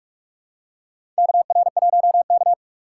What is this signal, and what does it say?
Morse code, heard as a keyed tone, about 46 wpm, 700 Hz. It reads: KR1K
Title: USA Call Signs - 4 Characters - 4x Character Spacing - 45wpm